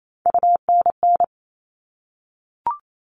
Morse code, heard as a keyed tone, about 28 wpm, 700 Hz. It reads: UNN
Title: All Cut Numbers - 3-Digit 28wpm